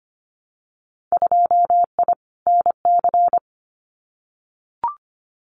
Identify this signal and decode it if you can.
Morse code, heard as a keyed tone, about 25 wpm, 700 Hz. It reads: 2I NC
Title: Winter Field Day Exchanges 25wpm